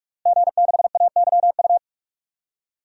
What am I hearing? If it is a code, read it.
Morse code, heard as a keyed tone, about 46 wpm, 700 Hz. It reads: G6AYU
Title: DXCC-100 Call Signs - 5 Characters - Repeated 3x - 45wpm